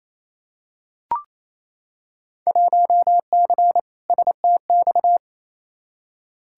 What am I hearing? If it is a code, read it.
Morse code, heard as a keyed tone, about 28 wpm, 700 Hz. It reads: 1C STX
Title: Field Day Exchanges Encore - 28wpm